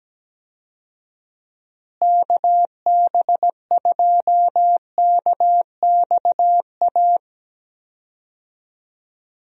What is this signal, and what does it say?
Morse code, heard as a keyed tone, about 17 wpm, 700 Hz. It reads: KB2KXA
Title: USA Call Signs - 6 Characters - Repeated 3x - 17wpm